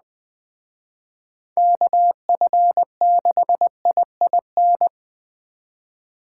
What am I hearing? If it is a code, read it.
Morse code, heard as a keyed tone, about 20 wpm, 700 Hz. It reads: KF6IIN
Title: Speed Racing - USA Call Signs 17wpm